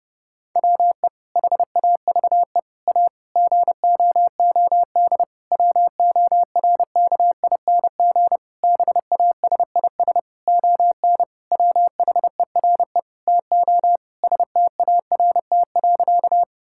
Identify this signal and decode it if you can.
Morse code, heard as a keyed tone, about 30 wpm, 700 Hz. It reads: WE HAVE A GOOD WORKING BASIS ON WHERE TO START.
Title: Sentences from Top 1000 Words - Review Part 5 - 30wpm